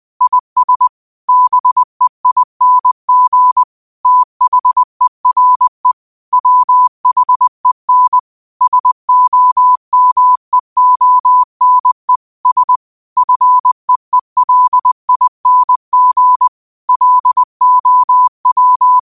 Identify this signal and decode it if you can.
Morse code, heard as a keyed tone, about 20 wpm, 1000 Hz. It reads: IS BEING THERE WHEN SOMEONE S FEELING LOW
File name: qotdfiles / 20wpm